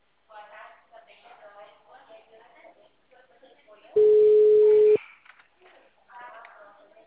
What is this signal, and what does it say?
Ringback tone (Europe, 425 Hz), with speech.